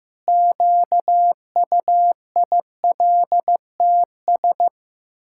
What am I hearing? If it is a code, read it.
Morse code, heard as a keyed tone, about 15 wpm, 700 Hz. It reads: QUILTS
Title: Pangram Challenge v2 - 15wpm